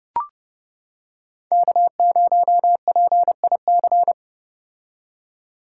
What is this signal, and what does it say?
Morse code, heard as a keyed tone, about 29 wpm, 700 Hz. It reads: K0PIC